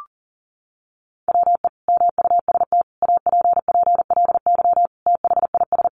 Morse code, heard as a keyed tone, about 40 wpm, 700 Hz. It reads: WE MUST APPLY THIS
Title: Sentences from Top 1400 Words 40wpm